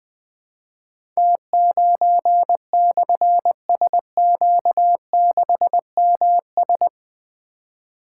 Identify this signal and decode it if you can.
Morse code, heard as a keyed tone, about 20 wpm, 700 Hz. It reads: T9/SQ6MS
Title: Difficult Call Signs 20wpm